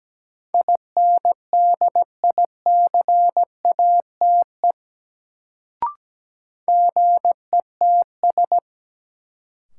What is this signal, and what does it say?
Morse code, heard as a keyed tone, about 17 wpm, 700 Hz. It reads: INDICATE GETS
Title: Sentences from Top 1300 Words - New Word Drill Alt - 17wpm